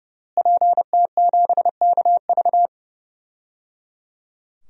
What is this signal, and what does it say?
Morse code, heard as a keyed tone, about 29 wpm, 700 Hz. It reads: PT7KV
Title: DXCC-100 Call Signs - 5 Characters - Repeated 3x - 30wpm